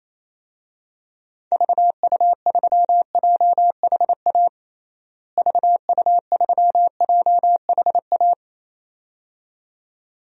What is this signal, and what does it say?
Morse code, heard as a keyed tone, about 28 wpm, 700 Hz. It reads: VU3JHA VU3JHA
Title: DXCC-100 Call Signs - 6 Characters - Repeated 2x - 28wpm